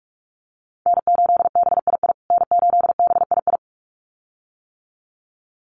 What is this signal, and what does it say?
Morse code, heard as a keyed tone, about 44 wpm, 690 Hz. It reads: N8BII N8BII
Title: USA Call Signs Repeated 2x 45wpm